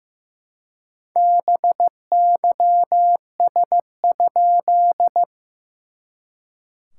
Morse code, heard as a keyed tone, about 15 wpm, 690 Hz. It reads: BYS?
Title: SOTA QSO Practice 15wpm